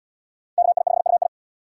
Morse code, heard as a keyed tone, about 68 wpm, 700 Hz. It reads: KI5UI